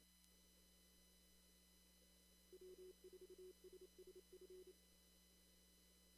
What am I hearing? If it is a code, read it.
Morse code, heard as a keyed tone, about 28 wpm, 370 Hz. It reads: W4SSF
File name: CW / kt5j wpx cw-op-k5pi